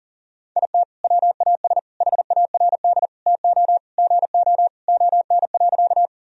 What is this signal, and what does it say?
Morse code, heard as a keyed tone, about 40 wpm, 700 Hz. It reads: IT WAS HARD TO GO ON.
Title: Sentences from Top 1000 Words - Review Part 1 - 40wpm